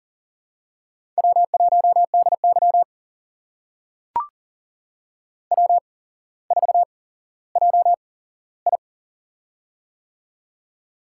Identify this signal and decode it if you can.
Morse code, heard as a keyed tone, about 42 wpm, spaced as for 17 wpm, 700 Hz. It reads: W1DY W4JI
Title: USA Call Signs - 4 Characters - 8x Character Spacing - 40wpm